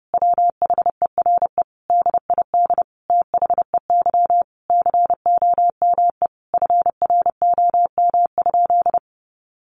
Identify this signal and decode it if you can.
Morse code, heard as a keyed tone, about 30 wpm, 700 Hz. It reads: WHERE DID THEY COME FROM?
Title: Sentences from Top 200 Words 30wpm